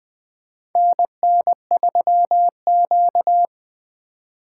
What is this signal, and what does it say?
Morse code, heard as a keyed tone, about 20 wpm, 700 Hz. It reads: NN3Q
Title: USA Call Signs - 4 Characters - 4x Character Spacing - 20wpm